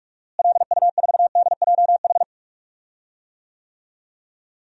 Morse code, heard as a keyed tone, about 46 wpm, 700 Hz. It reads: LU4DJH
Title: DXCC-100 Call Signs - 6 Characters - Repeated 2x - 45wpm